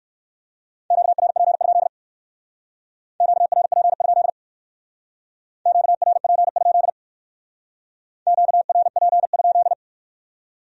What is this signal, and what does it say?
Morse code, heard as a keyed tone, about 56 wpm, 700 Hz. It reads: QRP? QRP? QRP? QRP?
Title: Speed-Racing Q-Codes - 45wpm